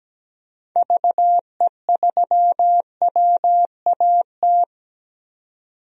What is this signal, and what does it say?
Morse code, heard as a keyed tone, about 17 wpm, 700 Hz. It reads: VE3WAT
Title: DXCC-100 Call Signs - 6 Characters - 17wpm